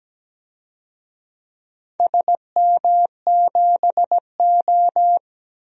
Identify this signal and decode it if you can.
Morse code, heard as a keyed tone, about 17 wpm, 700 Hz. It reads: SM7O
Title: DXCC-100 Call Signs - 4 Characters - Repeated 2x - 17wpm